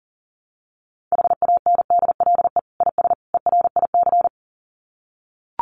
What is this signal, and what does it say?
Morse code, heard as a keyed tone, about 40 wpm, 700 Hz. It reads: HANDLE IS ERIC
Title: Common QSO Phrases 40wpm